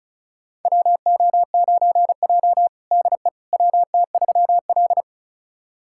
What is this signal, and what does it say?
Morse code, heard as a keyed tone, about 34 wpm, 700 Hz. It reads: WO9J DE WT3L
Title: USA Callsign DE Callsign - 4 Characters - 35wpm